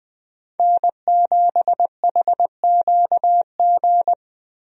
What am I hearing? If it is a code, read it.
Morse code, heard as a keyed tone, about 20 wpm, 700 Hz. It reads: N7HQG
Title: USA Call Signs - 5 Characters - Repeated 3x - 20wpm